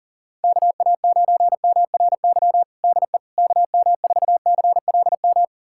Morse code, heard as a keyed tone, about 40 wpm, 700 Hz. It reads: KA9MRY DE KM4CLM
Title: USA Callsign DE Callsign - 6 Characters - 40wpm